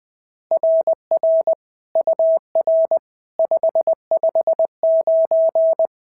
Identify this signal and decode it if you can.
Morse code, heard as a keyed tone, about 20 wpm, 690 Hz. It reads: RR UR 559
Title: SOTA QSO Practice 20wpm